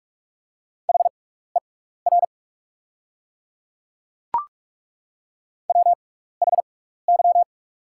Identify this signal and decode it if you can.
Morse code, heard as a keyed tone, about 48 wpm, spaced as for 16 wpm, 690 Hz. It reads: HER WHY
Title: 6x Character Spacing - Top 500 Words - 45wpm